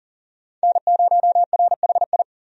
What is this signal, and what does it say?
Morse code, heard as a keyed tone, about 40 wpm, 700 Hz. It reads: N0RHI
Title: USA Call Signs - 5 Characters - 40wpm